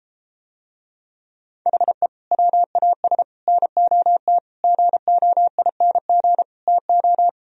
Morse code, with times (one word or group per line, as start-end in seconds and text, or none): 1.7-2.1 HE
2.3-3.2 WAS
3.5-4.4 NOT
4.6-6.4 GOING
6.7-7.3 TO